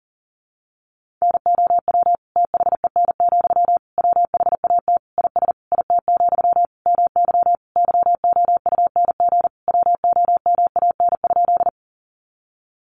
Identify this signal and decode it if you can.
Morse code, heard as a keyed tone, about 40 wpm, 700 Hz. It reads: NOW THEN, WHAT IS IT, MY YOUNG WOMAN?
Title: Sentences from Top 1100 Words 40wpm